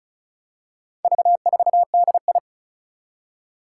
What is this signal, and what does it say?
Morse code, heard as a keyed tone, about 36 wpm, 700 Hz. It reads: V4DI
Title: Four Challenging Letter-Number Combinations 35wpm